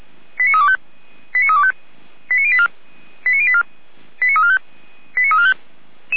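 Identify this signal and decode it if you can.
Five-tone selective call (ZVEI), heard as quick runs of five tones, tones of about 80 ms. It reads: 89325 89325 89084 89084 89345 89345
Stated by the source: ZVEI five-tone call received by SDR.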